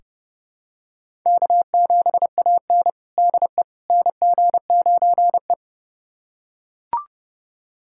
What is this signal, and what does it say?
Morse code, heard as a keyed tone, about 30 wpm, 700 Hz. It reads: K7AN DE NG9E
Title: USA Callsign DE Callsign - 4 Characters - 30wpm